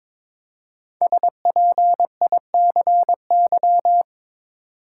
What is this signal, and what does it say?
Morse code, heard as a keyed tone, about 22 wpm, 700 Hz. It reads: SPICY
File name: 5 Letter Words - First 3 Letters Given - 22wpm